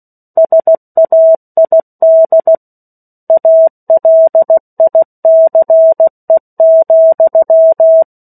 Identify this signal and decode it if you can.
Morse code, heard as a keyed tone, about 16 wpm, 650 Hz. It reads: SAID ALICE,
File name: 03-Chapter-03 16 WPM